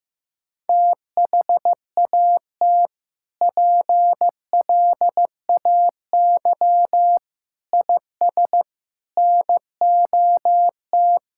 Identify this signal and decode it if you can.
Morse code, heard as a keyed tone, about 15 wpm, 700 Hz. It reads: THAT PLAY IS NOT
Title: Sentences from Top 1000 Words - Review Part 5 - 15wpm